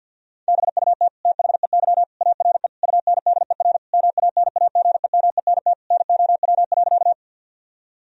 Morse code, heard as a keyed tone, about 50 wpm, 700 Hz. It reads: BUT THEY ARE UNDER MANAGEMENT NOW.
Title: Sentences from Top 1000 Words - Review Part 4 - 50wpm